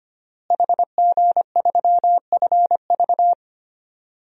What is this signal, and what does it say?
Morse code, heard as a keyed tone, about 25 wpm, 700 Hz. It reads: HG3FV
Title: DXCC-100 Call Signs - 5 Characters - Repeated 3x - 25wpm